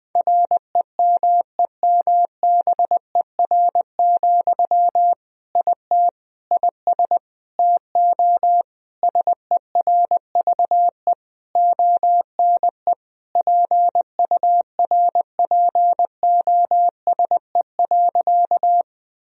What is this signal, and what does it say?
Morse code, heard as a keyed tone, about 20 wpm, 700 Hz. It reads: REMEMBER, IT IS TO SERVE ONE PURPOSE.
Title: Sentences from Top 1000 Words - Review Part 2 - 20wpm